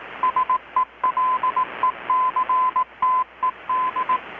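Morse code, heard as a keyed tone, about 18 wpm, 1020 Hz, with radio noise. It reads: SELECTED
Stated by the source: Morse from KPH's Night of Nights XX maritime radiotelegraphy event, received in Chiapas, Mexico about 3,800 km away on a Kenwood R-600 with a grounded 10 m long-wire antenna.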